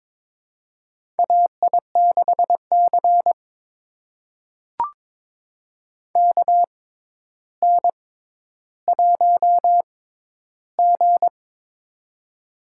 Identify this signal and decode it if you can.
Morse code, heard as a keyed tone, about 22 wpm, spaced as for 11 wpm, 700 Hz. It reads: AI6C KN1G
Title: USA Call Signs - 4 Characters - 6x Character Spacing - 22wpm